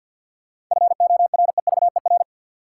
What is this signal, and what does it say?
Morse code, heard as a keyed tone, about 53 wpm, 700 Hz. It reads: FOREVER
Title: Sentences from Top 1600 Words - New Word Drill Alt - 50wpm